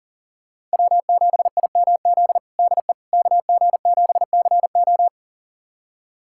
Morse code, heard as a keyed tone, about 40 wpm, 700 Hz. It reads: W7IMZ DE KG7CO
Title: USA Callsign DE Callsign - 5 Characters - 40wpm